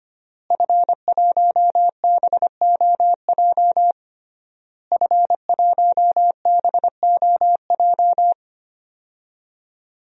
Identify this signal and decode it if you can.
Morse code, heard as a keyed tone, about 25 wpm, 700 Hz. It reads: F1BOJ F1BOJ
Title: DXCC-100 Call Signs - 5 Characters - Repeated 3x - 25wpm